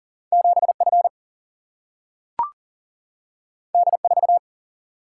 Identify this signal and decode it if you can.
Morse code, heard as a keyed tone, about 39 wpm, 700 Hz. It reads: 7 F D 4
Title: Single-Letter Single-Number 40wpm